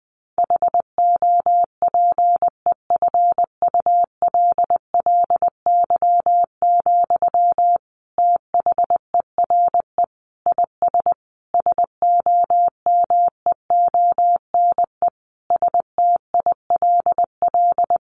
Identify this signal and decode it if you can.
Morse code, heard as a keyed tone, about 20 wpm, 700 Hz. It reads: HOPEFULLY, THERE IS SOMEONE STILL
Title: Sentences from Top 1500 Words 20wpm